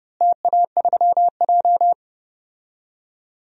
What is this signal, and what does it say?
Morse code, heard as a keyed tone, about 29 wpm, 700 Hz. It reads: TA3J
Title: DXCC-100 Call Signs - 4 Characters - Repeated 3x - 30wpm